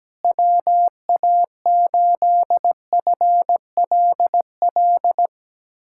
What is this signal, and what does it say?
Morse code, heard as a keyed tone, about 17 wpm, 700 Hz. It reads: WA8FLL
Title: USA Callsign DE Callsign - 6 Characters - 17wpm